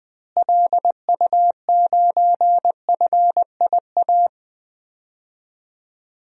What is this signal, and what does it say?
Morse code, heard as a keyed tone, about 20 wpm, 700 Hz. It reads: LU9FIA
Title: DXCC-100 Call Signs - 6 Characters - Repeated 3x - 20wpm